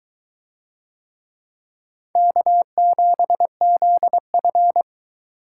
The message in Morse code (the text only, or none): K7ZF